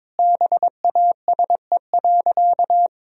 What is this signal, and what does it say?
Morse code, heard as a keyed tone, about 22 wpm, 700 Hz. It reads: BASE.
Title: Sentences from Top 1000 Words - Review Part 1 - 22wpm